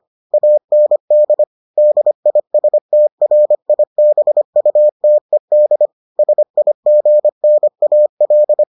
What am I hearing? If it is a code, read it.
Morse code, heard as a keyed tone, about 25 wpm, 600 Hz. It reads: AND DISTRIBUTED SIGNAL